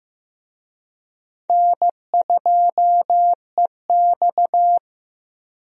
Morse code, heard as a keyed tone, about 15 wpm, 700 Hz. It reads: N2EX
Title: USA Call Signs - 4 Characters - 15wpm